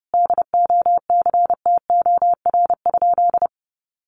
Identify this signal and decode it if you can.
Morse code, heard as a keyed tone, about 28 wpm, 700 Hz. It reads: DOCTOR?
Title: Sentences from Top 1500 Words 30wpm